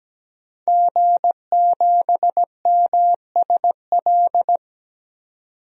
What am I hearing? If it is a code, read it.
Morse code, heard as a keyed tone, about 17 wpm, 700 Hz. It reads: G7MSL
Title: DXCC-100 Call Signs - 5 Characters - 17wpm